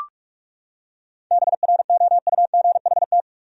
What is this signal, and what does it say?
Morse code, heard as a keyed tone, about 38 wpm, 700 Hz. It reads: BROUGHT
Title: Speed-Racing Top 500 Words 30wpm